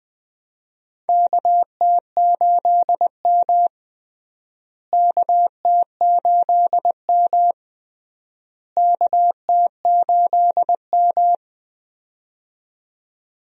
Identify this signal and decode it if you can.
Morse code, heard as a keyed tone, about 20 wpm, 700 Hz. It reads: KT8M KT8M KT8M
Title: USA Call Signs - 4 Characters - Repeated 3x - 20wpm